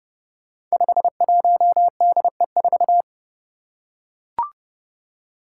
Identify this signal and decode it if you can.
Morse code, heard as a keyed tone, about 30 wpm, 700 Hz. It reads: 51DE4
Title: Five Challenging Letter-Number Combinations 30wpm